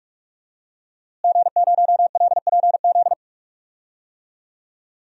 Morse code, heard as a keyed tone, about 42 wpm, 700 Hz. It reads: G0LPZ
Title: DXCC-100 Call Signs - 5 Characters - Repeated 3x - 45wpm